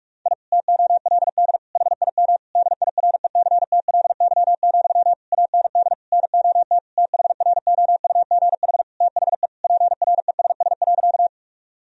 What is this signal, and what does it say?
Morse code, heard as a keyed tone, about 46 wpm, 700 Hz. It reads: I TOLD HIM DIRECTLY, AND NOT THROUGH THE PRESS.
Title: Sentences from Top 1000 Words - Review Part 1 - 45wpm